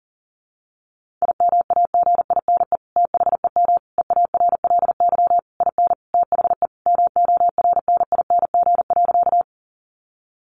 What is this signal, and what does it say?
Morse code, heard as a keyed tone, about 37 wpm, 700 Hz. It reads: IMAGINE THEM EARLY IN THE MORNING.
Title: Sentences from Top 1400 Words 40wpm